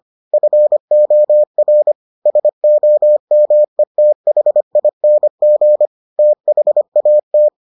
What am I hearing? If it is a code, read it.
Morse code, heard as a keyed tone, about 25 wpm, 600 Hz. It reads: FOR SOMETHING THAT